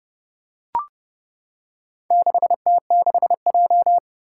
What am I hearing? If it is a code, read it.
Morse code, heard as a keyed tone, about 30 wpm, 690 Hz. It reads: 6T6J
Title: Four Challenging Letter-Number Combinations 30wpm